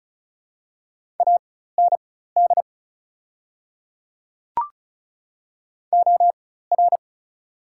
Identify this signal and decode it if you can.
Morse code, heard as a keyed tone, about 36 wpm, spaced as for 16 wpm, 690 Hz. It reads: AND OR
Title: 4x Character Spacing - Top 100 Words - 35wpm